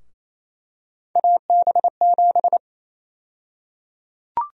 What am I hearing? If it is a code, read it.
Morse code, heard as a keyed tone, about 28 wpm, 700 Hz. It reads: AB7
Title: USA Call Sign Prefixes - Repeated 2x - 28wpm